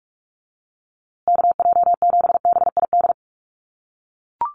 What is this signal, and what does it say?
Morse code, heard as a keyed tone, about 44 wpm, 700 Hz. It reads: KJ7BID
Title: USA Call Signs Repeated 3x 45wpm